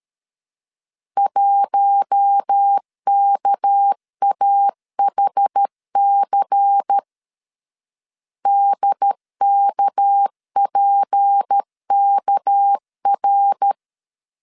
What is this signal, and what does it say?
Morse code, heard as a keyed tone, about 13 wpm, 800 Hz. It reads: 1KAHC DKPKR
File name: LEZIONI / 6 PKX0 / 6 PK 12wpm 2006